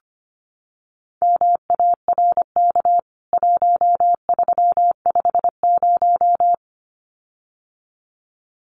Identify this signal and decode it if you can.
Morse code, heard as a keyed tone, about 25 wpm, 700 Hz. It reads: MARK 1350
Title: CWT Exchange 25wpm